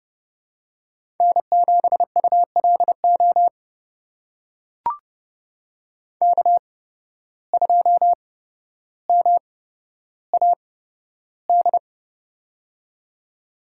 Morse code, heard as a keyed tone, about 32 wpm, spaced as for 13 wpm, 700 Hz. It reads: N7ULO K2MAD
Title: USA Call Signs - 5 Characters - 8x Character Spacing - 30wpm